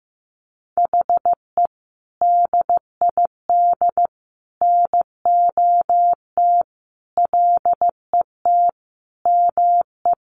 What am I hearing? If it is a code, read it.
Morse code, heard as a keyed tone, about 15 wpm, 700 Hz. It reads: HE DID NOT LET ME
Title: Sentences from Top 1200 Words 15wpm